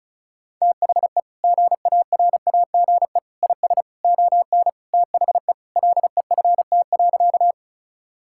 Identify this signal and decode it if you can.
Morse code, heard as a keyed tone, about 36 wpm, 700 Hz. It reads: THE GARAGE IS ON THE LEFT.
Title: Sentences from Top 1600 Words 35wpm